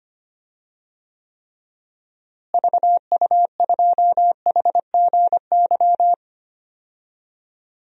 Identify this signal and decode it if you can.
Morse code, heard as a keyed tone, about 25 wpm, 700 Hz. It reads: VU2HGY
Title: DXCC-100 Call Signs - 6 Characters - Repeated 3x - 25wpm